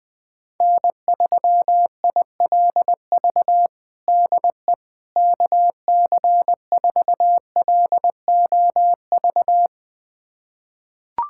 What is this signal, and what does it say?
Morse code, heard as a keyed tone, about 20 wpm, 700 Hz. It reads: N3ILV DE KC4LOV
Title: USA Callsign DE Callsign 20wpm